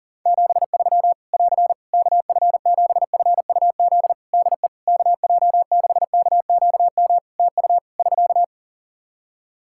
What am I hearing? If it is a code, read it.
Morse code, heard as a keyed tone, about 40 wpm, 700 Hz. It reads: 73 <AR> KF7FUZ DE KJ6KQM TU <SK>
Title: Practice QSO - Exchange 4 - Ending - 40wpm